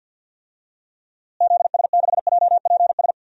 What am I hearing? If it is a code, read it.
Morse code, heard as a keyed tone, about 47 wpm, 700 Hz. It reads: ZS6JPS
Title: DXCC-100 Call Signs - 6 Characters - Repeated 3x - 50wpm